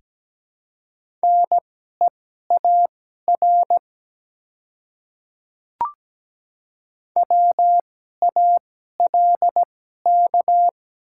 Morse code, heard as a keyed tone, about 17 wpm, spaced as for 12 wpm, 690 Hz. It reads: NEAR WALK
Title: 2x Character Spacing - Top 300 Words - 17wpm